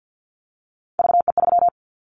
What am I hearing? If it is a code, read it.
Morse code, heard as a keyed tone, about 49 wpm, 700 Hz. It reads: VE3
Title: Callsign Prefixes for Top 100 DXCC - 50wpm